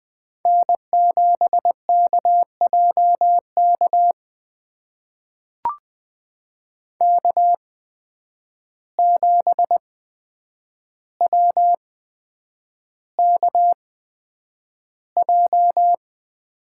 Morse code, heard as a keyed tone, about 20 wpm, spaced as for 9 wpm, 700 Hz. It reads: N7KJK K7WKJ
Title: USA Call Signs - 5 Characters - 8x Character Spacing - 20wpm